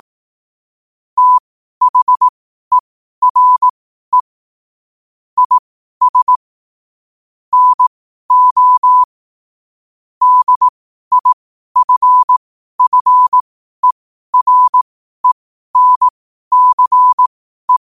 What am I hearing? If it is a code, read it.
Morse code, heard as a keyed tone, about 18 wpm, spaced as for 12 wpm, 1000 Hz. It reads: THERE IS NO DIFFERENCE
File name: qotdfiles / 12wpm